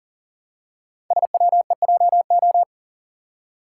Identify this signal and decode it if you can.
Morse code, heard as a keyed tone, about 40 wpm, 700 Hz. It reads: SWEJO
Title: Code Groups - 5 Letters - 40wpm